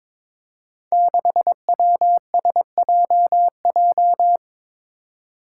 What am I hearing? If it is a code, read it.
Morse code, heard as a keyed tone, about 22 wpm, 700 Hz. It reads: 6WSJJ
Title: Five Challenging Letter-Number Combinations 22wpm